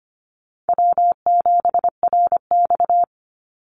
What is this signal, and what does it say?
Morse code, heard as a keyed tone, about 25 wpm, 700 Hz. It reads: W7RX
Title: Easy Call Signs Repeated 2x - 25wpm